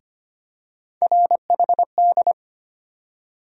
Morse code, heard as a keyed tone, about 25 wpm, 700 Hz. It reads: RHD
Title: USA Call Sign Postfixes - 3 Letters 25wpm